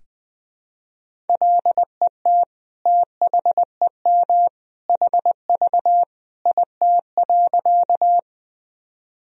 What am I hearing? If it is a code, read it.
Morse code, heard as a keyed tone, about 21 wpm, 700 Hz. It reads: LET THEM HV IT.
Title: Sentences from Top 300 Words and Abbreviations 20wpm